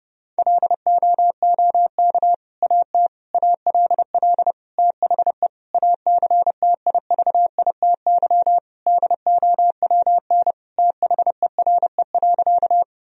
Morse code, heard as a keyed tone, about 32 wpm, 700 Hz. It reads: LOOK AT ALL THE ACTIVITY DOWN THERE.
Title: Sentences from Top 1000 Words - Review Part 4 - 30wpm